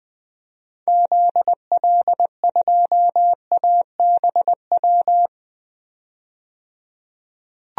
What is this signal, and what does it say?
Morse code, heard as a keyed tone, about 20 wpm, 700 Hz. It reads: ZL2ABW
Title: DXCC-100 Call Signs - 6 Characters - 20wpm